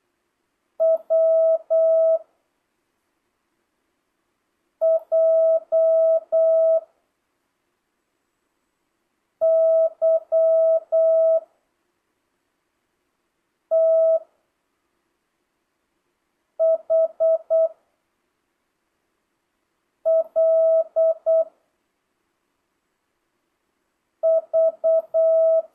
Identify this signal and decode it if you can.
Morse code, heard as a keyed tone, about 8 wpm, 1300 Hz. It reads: W J Y T H L V
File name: jh8cbh / 2025cw